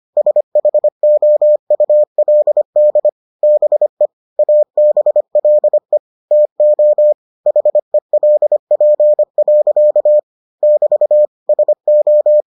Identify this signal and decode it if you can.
Morse code, heard as a keyed tone, about 25 wpm, 600 Hz. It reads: SHOULD BE ABLE TO HELP. <BT> SO